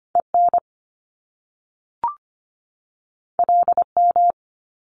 Morse code, heard as a keyed tone, about 25 wpm, 700 Hz. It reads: EN LM
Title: Top Two Letter Combinations in Words 25wpm